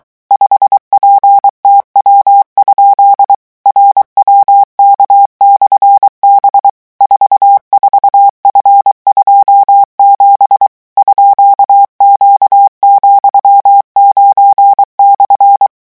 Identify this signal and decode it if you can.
Morse code, heard as a keyed tone, about 23 wpm, 800 Hz. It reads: 5PTW? RWK/B 44F27 _Q,9/
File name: morse mp3 / revision13-vitesse 22 mots